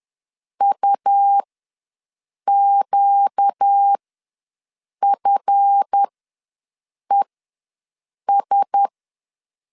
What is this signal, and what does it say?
Morse code, heard as a keyed tone, about 10 wpm, spaced as for 6 wpm, 800 Hz. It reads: UQFES
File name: LEZIONI / 7 GF9Z / 7 GF ws 10wpm 3918